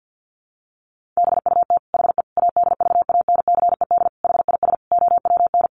Morse code, heard as a keyed tone, about 50 wpm, 700 Hz. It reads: BUT HE ADVANCED HIS OWN
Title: Sentences from Top 1200 Words 50wpm